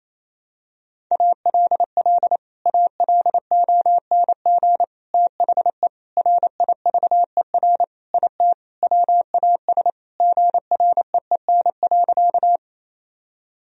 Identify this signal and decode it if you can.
Morse code, heard as a keyed tone, about 27 wpm, 700 Hz. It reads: ALL ALONG THE RIVER IT WAS GREEN.
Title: Sentences from Top 1000 Words - Review Part 2 - 28wpm